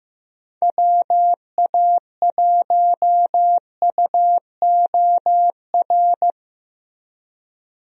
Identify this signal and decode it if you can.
Morse code, heard as a keyed tone, about 15 wpm, 700 Hz. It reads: WA1UOR
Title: USA Call Signs - 6 Characters - Repeated 3x - 15wpm